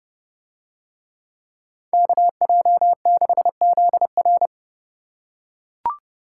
Morse code, heard as a keyed tone, about 28 wpm, 700 Hz. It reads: KJ6ZR